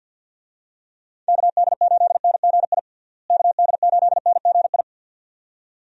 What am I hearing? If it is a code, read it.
Morse code, heard as a keyed tone, about 52 wpm, 700 Hz. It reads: KD8NGI KD8NGI
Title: USA Call Signs - 6 Characters - Repeated 2x - 50wpm